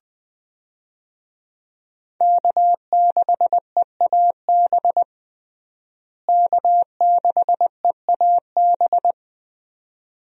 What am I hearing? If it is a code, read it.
Morse code, heard as a keyed tone, about 21 wpm, 700 Hz. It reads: K6EAB K6EAB
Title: USA Call Signs - 5 Characters - Repeated 2x - 20wpm